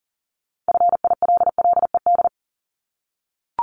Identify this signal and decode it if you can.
Morse code, heard as a keyed tone, about 40 wpm, 700 Hz. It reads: FILLED
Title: Multi-letter Phonemes 40wpm